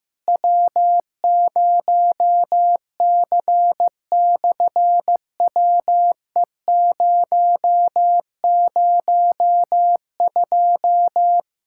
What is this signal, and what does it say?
Morse code, heard as a keyed tone, about 15 wpm, 700 Hz. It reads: W0C/WE002
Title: SOTA QSO Practice 15wpm